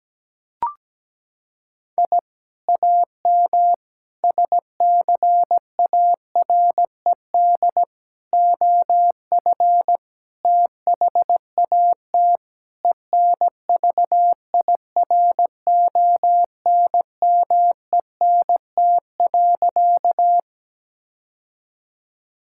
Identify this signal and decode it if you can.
Morse code, heard as a keyed tone, about 17 wpm, 700 Hz. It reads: I AM SCARED OF THAT ENVIRONMENT.
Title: Sentences from Top 1000 Words - Review Part 5 - 17wpm